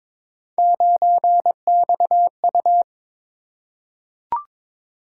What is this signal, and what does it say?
Morse code, heard as a keyed tone, about 21 wpm, 700 Hz. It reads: 9XU
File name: practice / 22wpm / Three Letter-Number Combinations - 22wpm